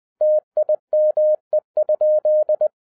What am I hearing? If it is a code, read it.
Morse code, heard as a keyed tone, about 20 wpm, 610 Hz. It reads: TIME?
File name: 2012-01-mp3 / n16-THROUGH EIGHT MONTHS-20wpm